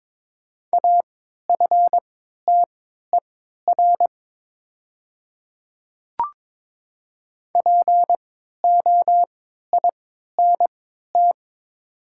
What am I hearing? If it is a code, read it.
Morse code, heard as a keyed tone, about 23 wpm, spaced as for 13 wpm, 700 Hz. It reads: AFTER POINT
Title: 3x Character Spacing - Top 300 Words - 22wpm